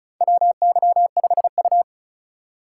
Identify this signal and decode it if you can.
Morse code, heard as a keyed tone, about 34 wpm, 700 Hz. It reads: WY5U
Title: USA Call Signs - 4 Characters - 4x Character Spacing - 35wpm